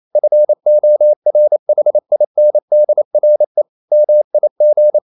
Morse code, heard as a keyed tone, about 28 wpm, 600 Hz. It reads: FORHINDRE MIG
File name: Ferie QRQ 28wpm